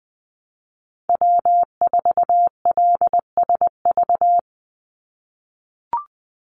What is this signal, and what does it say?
Morse code, heard as a keyed tone, about 20 wpm, 700 Hz. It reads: W4LSV
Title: Call Signs - 20wpm